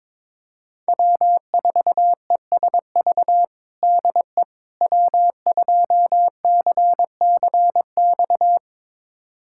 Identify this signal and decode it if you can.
Morse code, heard as a keyed tone, about 22 wpm, 700 Hz. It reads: W4ESV DE W2CCX
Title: USA Callsign DE Callsign - 5 Characters - 22wpm